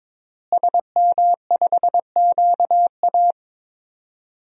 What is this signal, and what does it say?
Morse code, heard as a keyed tone, about 22 wpm, 700 Hz. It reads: SM5QA
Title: DXCC-100 Call Signs - 5 Characters - Repeated 2x - 22wpm